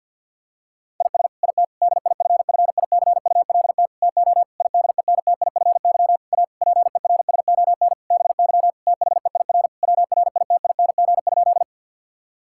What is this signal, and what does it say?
Morse code, heard as a keyed tone, about 50 wpm, 700 Hz. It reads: IS IT DIFFICULT TO IDENTIFY A PERSON BY THEIR WRITING?
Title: Sentences from Top 1000 Words - Review Part 6 - 50wpm